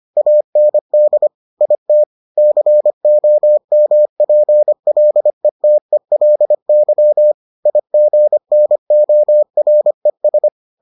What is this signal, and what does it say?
Morse code, heard as a keyed tone, about 25 wpm, 600 Hz. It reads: AND IT COMPLETELY IGNORES